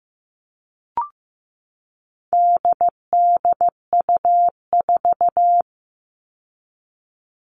Morse code, heard as a keyed tone, about 15 wpm, 700 Hz. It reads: DDU4
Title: All Cut Numbers - 4-Digit 15wpm